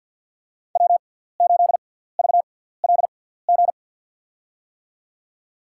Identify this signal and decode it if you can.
Morse code, heard as a keyed tone, about 52 wpm, spaced as for 17 wpm, 700 Hz. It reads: W8VLG
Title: USA Call Signs - 5 Characters - 6x Character Spacing - 50wpm